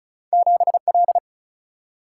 Morse code, heard as a keyed tone, about 36 wpm, 700 Hz. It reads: 7L
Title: Single-Letter Single-Number 35wpm